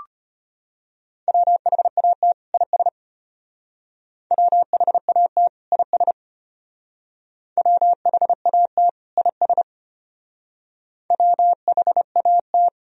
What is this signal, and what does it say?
Morse code, heard as a keyed tone, about 31 wpm, 700 Hz. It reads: WHAT IS WHAT IS WHAT IS WHAT
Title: Speed-Racing - Sets of 2 Words 25wpm